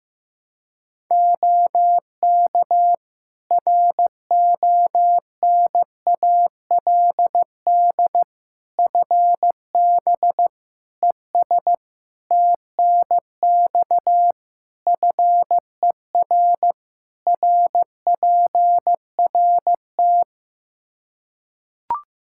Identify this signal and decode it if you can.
Morse code, heard as a keyed tone, about 15 wpm, 700 Hz. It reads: OK RONALD FB ES TNX FER RPRT
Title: Practice QSO - Exchange 2 - Info - 15wpm